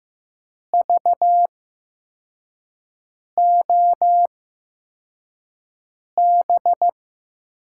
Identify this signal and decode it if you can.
Morse code, heard as a keyed tone, about 15 wpm, spaced as for 4 wpm, 700 Hz. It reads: VOB
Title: USA Call Signs - 6 Characters - 8x Character Spacing - 15wpm